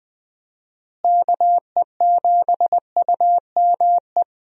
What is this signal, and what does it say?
Morse code, heard as a keyed tone, about 20 wpm, 700 Hz. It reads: KE7UME
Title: USA Call Signs - 6 Characters - Repeated 2x - 20wpm